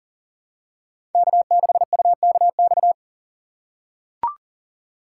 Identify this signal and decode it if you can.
Morse code, heard as a keyed tone, about 40 wpm, 700 Hz. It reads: K6UKX
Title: USA Call Signs - 5 Characters - 3x Character Spacing - 40wpm